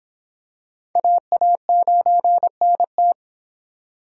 Morse code, heard as a keyed tone, about 26 wpm, 700 Hz. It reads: AA9NT